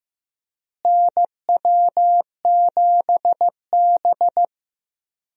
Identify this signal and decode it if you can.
Morse code, heard as a keyed tone, about 15 wpm, 700 Hz. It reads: NW7B
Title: USA Call Signs - 4 Characters - Repeated 2x - 15wpm